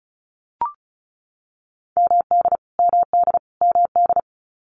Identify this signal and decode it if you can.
Morse code, heard as a keyed tone, about 34 wpm, 700 Hz. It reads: MD MD MD
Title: US States Repeated 3x 35wpm